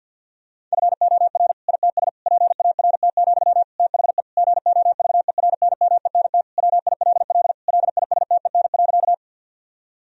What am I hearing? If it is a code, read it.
Morse code, heard as a keyed tone, about 50 wpm, 700 Hz. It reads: FOR ITS PART, THE GOVERNMENT WILL LISTEN.
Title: Sentences from Top 1000 Words - Review Part 3 - 50wpm